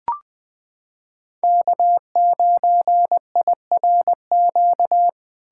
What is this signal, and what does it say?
Morse code, heard as a keyed tone, about 21 wpm, 700 Hz. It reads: K9IRQ
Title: USA Callsign DE Callsign - 5 Characters - 20wpm